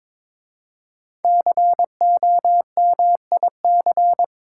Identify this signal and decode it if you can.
Morse code, heard as a keyed tone, about 23 wpm, 700 Hz. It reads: COMIC
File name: 5 Letter Words - First 3 Letters Given - 22wpm